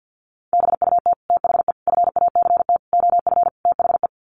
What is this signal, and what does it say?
Morse code, heard as a keyed tone, about 50 wpm, 700 Hz. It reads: BUT THE FACT OF THE
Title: Sets of 5 Words Encore2 50wpm